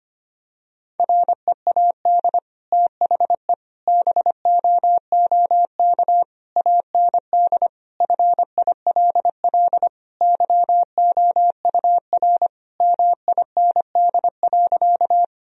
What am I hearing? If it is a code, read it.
Morse code, heard as a keyed tone, about 25 wpm, 700 Hz. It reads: READ THE BOOK AND FILL YOUR MIND.
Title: Sentences from Top 1000 Words - Review Part 3 - 25wpm